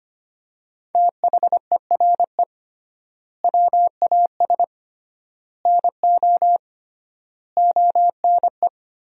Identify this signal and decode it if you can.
Morse code, heard as a keyed tone, about 25 wpm, 700 Hz. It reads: THERE WAS NO ONE
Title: 3x Word-Spacing - Sentences from Top 200 Words - 25wpm